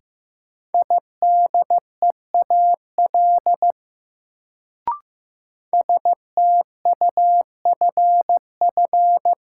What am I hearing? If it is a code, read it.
Morse code, heard as a keyed tone, about 15 wpm, 700 Hz. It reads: IDEAL STUFF
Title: Sentences from Top 1400 Words - New Word Drill - 15wpm